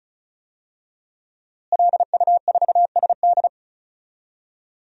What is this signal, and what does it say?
Morse code, heard as a keyed tone, about 36 wpm, 700 Hz. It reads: LU4SD
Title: DXCC-100 Call Signs - 5 Characters - Repeated 2x - 35wpm